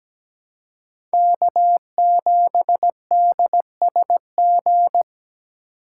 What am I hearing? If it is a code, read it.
Morse code, heard as a keyed tone, about 17 wpm, 700 Hz. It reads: K7DSG
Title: USA Call Signs - 2x Character Spacing - 17wpm